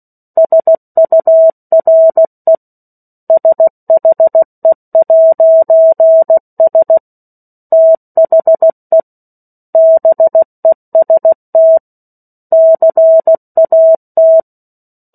Morse code, heard as a keyed tone, about 16 wpm, 650 Hz. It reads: SURE SHE'S THE BEST CAT
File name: 03-Chapter-03 16 WPM